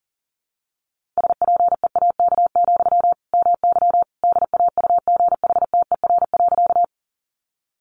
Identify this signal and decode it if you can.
Morse code, heard as a keyed tone, about 40 wpm, 700 Hz. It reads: SPEAK, MY DAUGHTER.
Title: Sentences from Top 1500 Words 40wpm